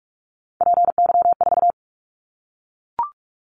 Morse code, heard as a keyed tone, about 44 wpm, 700 Hz. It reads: PY4
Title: Callsign Prefixes for Top 100 DXCC - 45wpm